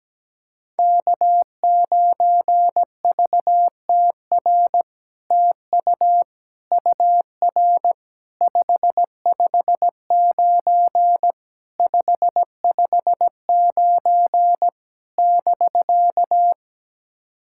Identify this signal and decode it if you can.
Morse code, heard as a keyed tone, about 17 wpm, 700 Hz. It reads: K9VTR TU UR 559 559 <BK>
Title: SOTA QSO Practice 17wpm